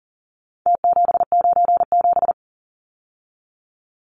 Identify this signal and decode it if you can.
Morse code, heard as a keyed tone, about 40 wpm, 700 Hz. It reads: T797
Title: Numbers - Contest Style - 4-Digit 40wpm